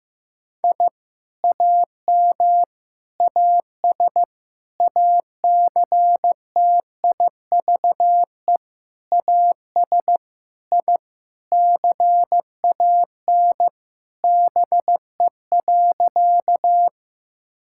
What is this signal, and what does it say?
Morse code, heard as a keyed tone, about 15 wpm, 700 Hz. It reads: I AM AS ACTIVE AS I CAN BE.
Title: Sentences from Top 1000 Words - Review Part 6 - 15wpm